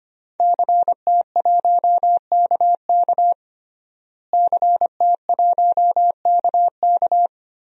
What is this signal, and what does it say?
Morse code, heard as a keyed tone, about 25 wpm, 700 Hz. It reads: CT1KK CT1KK
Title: DXCC-100 Call Signs - 5 Characters - Repeated 3x - 25wpm